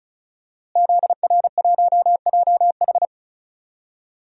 Morse code, heard as a keyed tone, about 36 wpm, 700 Hz. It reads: ZR1JH
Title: DXCC-100 Call Signs - 5 Characters - Repeated 2x - 35wpm